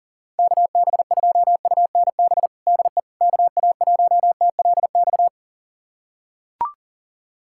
Morse code, heard as a keyed tone, about 40 wpm, 700 Hz. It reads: KB2UNB DE KA1TLX
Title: USA Callsign DE Callsign - 6 Characters - 40wpm